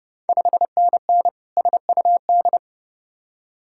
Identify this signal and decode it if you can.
Morse code, heard as a keyed tone, about 30 wpm, 700 Hz. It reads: 5NN SUD
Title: Ontario QSO Party Exchange 30wpm